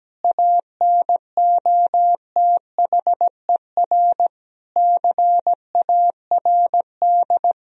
Morse code, heard as a keyed tone, about 17 wpm, 700 Hz. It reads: ANOTHER CARD
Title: Sentences from Top 1000 Words - Review Part 4 - 17wpm